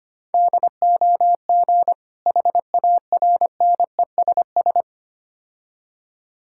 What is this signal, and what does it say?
Morse code, heard as a keyed tone, about 25 wpm, 700 Hz. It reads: DOG HARNESS
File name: practice / 25wpm / Difficult Words with Hints - 25wpm